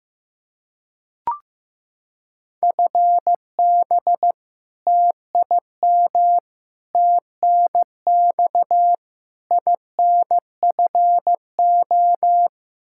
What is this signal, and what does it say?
Morse code, heard as a keyed tone, about 16 wpm, 700 Hz. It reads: FB TIM TNX INFO
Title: K1USN SST QSO Practice 15wpm